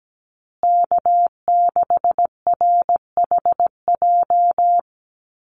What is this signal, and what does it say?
Morse code, heard as a keyed tone, about 17 wpm, 700 Hz. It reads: K6RHJ
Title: USA Call Signs Repeated 3x 17wpm